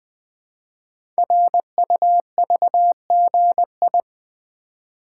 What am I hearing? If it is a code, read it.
Morse code, heard as a keyed tone, about 20 wpm, 700 Hz. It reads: RUVGI
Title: Code Groups - 5 Letter:Numbers - 20wpm